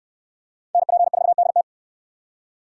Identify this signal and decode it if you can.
Morse code, heard as a keyed tone, about 68 wpm, 700 Hz. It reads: NP4DT